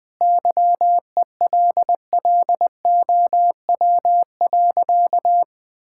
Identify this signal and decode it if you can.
Morse code, heard as a keyed tone, about 20 wpm, 700 Hz. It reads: YELLOW.
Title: Easy Sentences with Hints - 20wpm